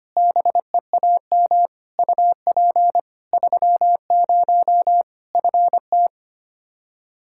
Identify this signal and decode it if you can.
Morse code, heard as a keyed tone, about 25 wpm, 700 Hz. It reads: BEAM UP 30 FT
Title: Practice QSO - Exchange 2 - Info - 25wpm